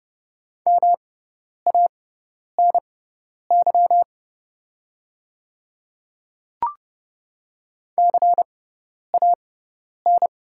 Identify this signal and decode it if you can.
Morse code, heard as a keyed tone, about 30 wpm, spaced as for 10 wpm, 690 Hz. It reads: MANY CAN
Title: 6x Character Spacing - Top 500 Words - 30wpm